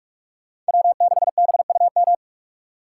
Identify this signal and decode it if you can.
Morse code, heard as a keyed tone, about 58 wpm, 700 Hz. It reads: W6BUM